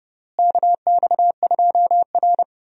Morse code, heard as a keyed tone, about 30 wpm, 700 Hz. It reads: KX2R
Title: USA Call Signs - 4 Characters - Repeated 3x - 30wpm